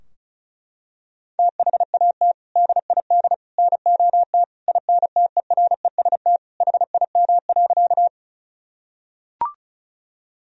Morse code, heard as a keyed tone, about 36 wpm, 700 Hz. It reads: THAT DID NOT INTEREST HIM.
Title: Sentences from Top 1000 Words - Review Part 2 - 35wpm